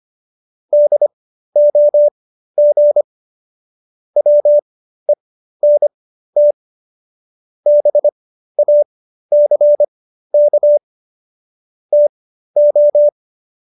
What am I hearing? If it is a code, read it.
Morse code, heard as a keyed tone, about 25 wpm, spaced as for 13 wpm, 590 Hz. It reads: DOG WENT BACK TO